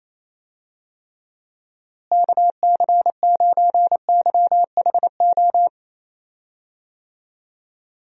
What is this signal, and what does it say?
Morse code, heard as a keyed tone, about 28 wpm, 700 Hz. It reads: KC9YHO
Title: USA Call Signs - 6 Characters - Repeated 2x - 28wpm